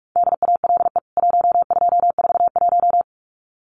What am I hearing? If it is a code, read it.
Morse code, heard as a keyed tone, about 44 wpm, 700 Hz. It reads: DALE 1241
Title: CWT Exchange 45wpm